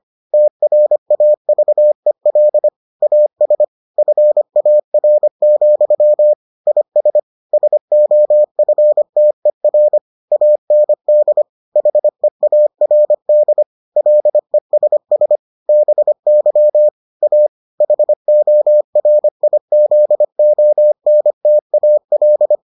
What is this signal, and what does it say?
Morse code, heard as a keyed tone, about 25 wpm, 600 Hz. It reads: TRAVEL AS FAR, IS SOFTER AND HEARD LESS BY A HORIZONTAL